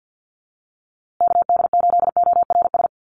Morse code, heard as a keyed tone, about 50 wpm, 700 Hz. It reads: KD8ORS
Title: USA Call Signs Repeated 2x 50wpm